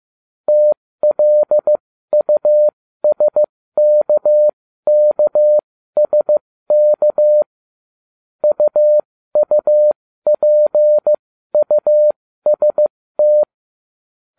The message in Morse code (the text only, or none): TLUSKKSK UUPUST